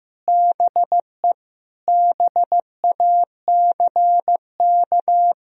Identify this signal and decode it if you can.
Morse code, heard as a keyed tone, about 15 wpm, 700 Hz. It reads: BE BACK
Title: Sentences from Top 1000 Words - Review Part 2 - 15wpm